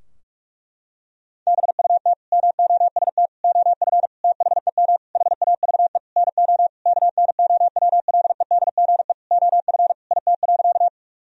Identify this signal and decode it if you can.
Morse code, heard as a keyed tone, about 46 wpm, 700 Hz. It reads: BUT MOST OF THEM HAVE NO KNOWLEDGE OF IT.
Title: Sentences from Top 1000 Words - Review Part 3 - 45wpm